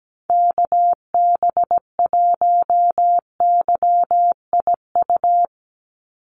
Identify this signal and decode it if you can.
Morse code, heard as a keyed tone, about 17 wpm, 700 Hz. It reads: KB1YIU
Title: USA Call Signs 17wpm